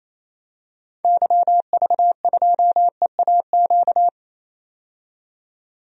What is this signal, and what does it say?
Morse code, heard as a keyed tone, about 28 wpm, 700 Hz. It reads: YV2EAQ
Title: DXCC-100 Call Signs - 6 Characters - 28wpm